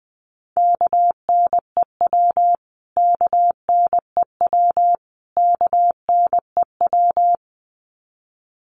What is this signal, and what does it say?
Morse code, heard as a keyed tone, about 20 wpm, 700 Hz. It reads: KNEW KNEW KNEW
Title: Top 500 Words Repeated 20wpm